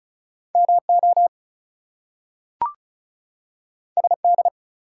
Morse code, heard as a keyed tone, about 36 wpm, 700 Hz. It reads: MO SD
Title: US States and Canadian Provinces 35wpm